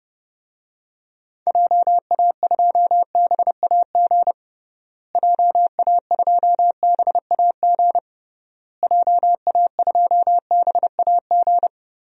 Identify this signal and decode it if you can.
Morse code, heard as a keyed tone, about 30 wpm, 700 Hz. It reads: JA2BAG JA2BAG JA2BAG
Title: DXCC-100 Call Signs - 6 Characters - Repeated 3x - 30wpm